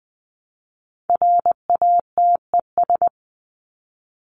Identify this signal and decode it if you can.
Morse code, heard as a keyed tone, about 20 wpm, 700 Hz. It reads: RATES
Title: ES:S Suffix - Top 100 Words with Base Words 20wpm